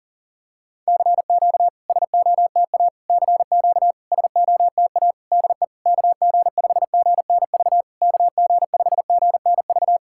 Morse code, heard as a keyed tone, about 40 wpm, 690 Hz. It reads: CQ SOTA CQ SOTA DE KG5GNV KG5GNV
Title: SOTA QSO Practice 40wpm